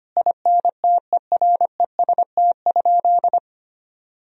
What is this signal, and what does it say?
Morse code, heard as a keyed tone, about 25 wpm, 700 Hz. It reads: INTEREST?
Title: Sentences from Top 1000 Words - Review Part 2 - 25wpm